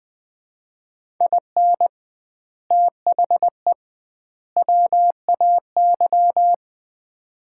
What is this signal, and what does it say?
Morse code, heard as a keyed tone, about 20 wpm, 700 Hz. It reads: IN THE WAY
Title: 2x Word-Spacing - Sets of 3 Words - 20wpm